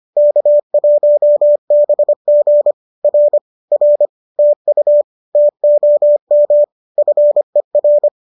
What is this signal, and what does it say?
Morse code, heard as a keyed tone, about 25 wpm, 590 Hz. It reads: K1BG R R TU TOM FER
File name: qso201 25